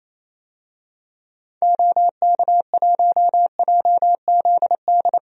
Morse code, heard as a keyed tone, about 28 wpm, 700 Hz. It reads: OK1JZD
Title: DXCC-100 Call Signs - 6 Characters - Repeated 2x - 28wpm